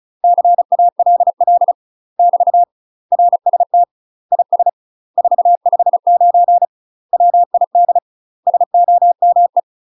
Morse code, heard as a keyed tone, about 34 wpm, 700 Hz. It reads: CALL <BT> RST IS 459 WID SOME
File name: UK MASTER QSOs TRAINING / 35WPM / UK MASTER QSOs 0024